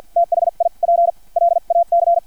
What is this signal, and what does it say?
Morse code, heard as a keyed tone, about 50 wpm, spaced as for 38 wpm, 680 Hz. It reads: THI PLAC
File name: archived / CRSnet